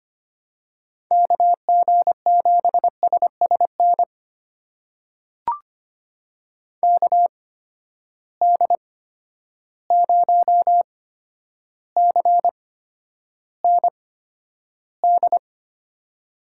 Morse code, heard as a keyed tone, about 25 wpm, spaced as for 11 wpm, 700 Hz. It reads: KG7SSN KD0CND
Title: USA Call Signs - 6 Characters - 8x Character Spacing - 25wpm